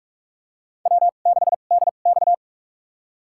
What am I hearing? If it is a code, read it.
Morse code, heard as a keyed tone, about 41 wpm, spaced as for 31 wpm, 700 Hz. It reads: W6DX
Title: USA Call Signs - 4 Characters - 2x Character Spacing - 45wpm